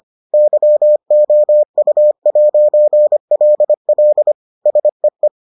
Morse code, heard as a keyed tone, about 25 wpm, 600 Hz. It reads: YOU'LL SEE